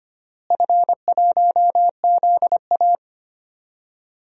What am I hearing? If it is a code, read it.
Morse code, heard as a keyed tone, about 25 wpm, 700 Hz. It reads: F1ZA
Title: DXCC-100 Call Signs - 4 Characters - Repeated 3x - 25wpm